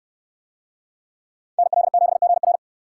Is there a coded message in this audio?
Yes: Morse code from 1.6 to 2.6 s, heard as a keyed tone, about 69 wpm, 700 Hz. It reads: NL7GR